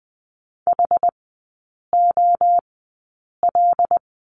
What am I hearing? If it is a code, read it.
Morse code, heard as a keyed tone, about 20 wpm, spaced as for 8 wpm, 700 Hz. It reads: HOL
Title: 074 Lesson 13e - Letter L - Three Characters 20wpm